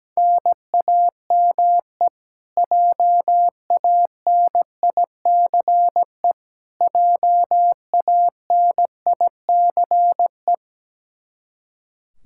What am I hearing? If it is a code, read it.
Morse code, heard as a keyed tone, about 17 wpm, 700 Hz. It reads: NAME JANICE JANICE
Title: Practice QSO - Exchange 1 - Beginning - 17wpm